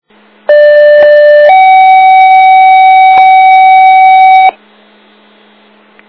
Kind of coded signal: two-tone page